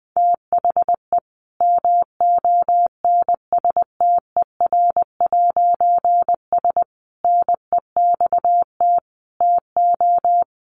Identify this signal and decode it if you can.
Morse code, heard as a keyed tone, about 19 wpm, 700 Hz. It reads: THE MONSTER'S NEXT TO
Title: A Sound Of Thunder 20wpm